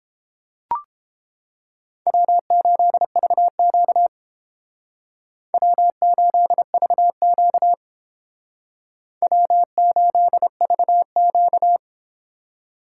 Morse code, heard as a keyed tone, about 32 wpm, 700 Hz. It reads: W8VQ W8VQ W8VQ